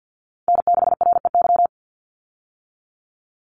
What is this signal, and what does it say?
Morse code, heard as a keyed tone, about 50 wpm, 700 Hz. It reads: N6REY
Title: USA Call Signs 50wpm